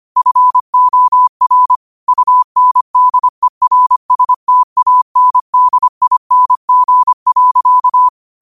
Morse code, heard as a keyed tone, about 25 wpm, 1000 Hz. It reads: FOR UNDERSTANDING.
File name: qotdfiles / 25wpm